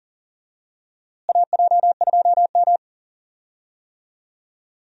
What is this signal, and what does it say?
Morse code, heard as a keyed tone, about 38 wpm, 700 Hz. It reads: AJ2M
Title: USA Call Signs - 4 Characters - Repeated 3x - 40wpm